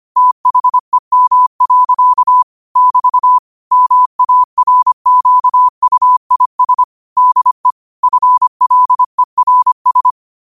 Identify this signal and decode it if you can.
Morse code, heard as a keyed tone, about 25 wpm, 1000 Hz. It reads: THEM. <BT> MARQUIS DE FLERS
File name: qotdfiles / 25wpm